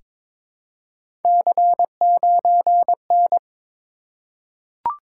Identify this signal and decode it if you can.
Morse code, heard as a keyed tone, about 22 wpm, 700 Hz. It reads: C9N
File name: practice / 22wpm / Three Letter-Number Combinations - 22wpm